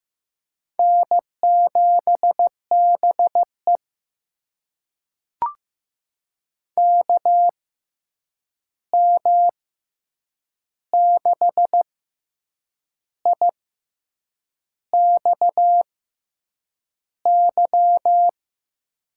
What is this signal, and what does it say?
Morse code, heard as a keyed tone, about 15 wpm, spaced as for 7 wpm, 700 Hz. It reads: N7BE KM6IXY
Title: USA Call Signs - 6x Character Spacing - 15wpm